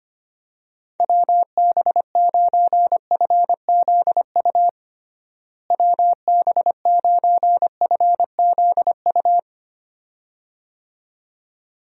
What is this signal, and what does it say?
Morse code, heard as a keyed tone, about 25 wpm, 700 Hz. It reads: WB9FZU WB9FZU
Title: USA Call Signs - 6 Characters - Repeated 2x - 25wpm